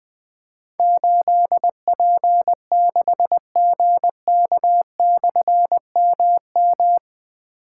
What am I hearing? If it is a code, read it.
Morse code, heard as a keyed tone, about 20 wpm, 700 Hz. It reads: 8P6GK/MM
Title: Difficult Call Signs 20wpm